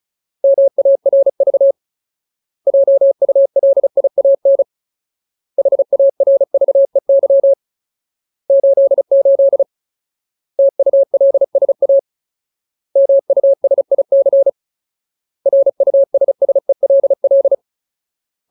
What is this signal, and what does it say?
Morse code, heard as a keyed tone, about 36 wpm, 550 Hz. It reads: MARV JULIAN HARVEY 88 TULSA MUSIC RUSSELL
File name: Ultimate Word List / 35WPM Ultimate